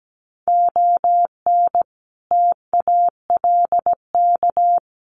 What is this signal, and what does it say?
Morse code, heard as a keyed tone, about 17 wpm, 700 Hz. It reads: ON TALK
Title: Sets of 5 Words 17wpm